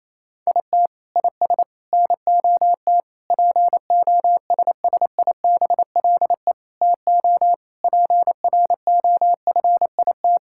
Morse code, heard as a keyed tone, about 28 wpm, 700 Hz. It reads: IT IS NOT POSSIBLE TO PROFIT
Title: Sentences from Top 1000 Words - Review Part 5 - 28wpm